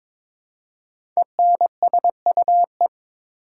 Morse code, heard as a keyed tone, about 22 wpm, 700 Hz. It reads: ENSUE
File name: 5 Letter Words - First 3 Letters Given - 22wpm